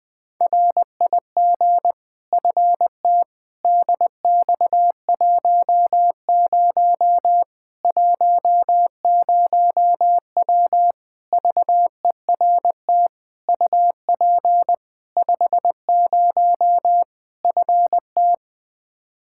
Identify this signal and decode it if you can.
Morse code, heard as a keyed tone, about 20 wpm, 700 Hz. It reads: RIG FT DX10 10W VERT UP 50 FT
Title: Practice QSO - Exchange 2 - Info - 20wpm